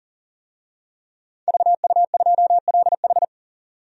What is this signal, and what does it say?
Morse code, heard as a keyed tone, about 39 wpm, 700 Hz. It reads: VU2LH
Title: DXCC-100 Call Signs - 5 Characters - 40wpm